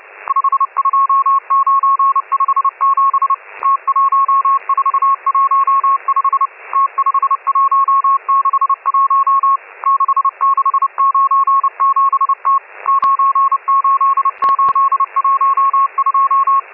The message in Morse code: 52957 T1415 T5161 6697T 98812